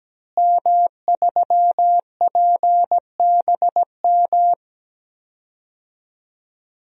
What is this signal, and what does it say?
Morse code, heard as a keyed tone, about 17 wpm, 700 Hz. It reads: M3PBM
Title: DXCC-100 Call Signs - 5 Characters - Repeated 2x - 17wpm